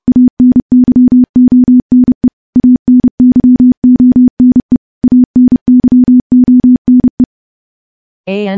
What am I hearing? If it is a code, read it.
Morse code, heard as a keyed tone, about 31 wpm, 250 Hz. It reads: ANYONE ANYONE ANYONE
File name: morse / morse-training-30WPM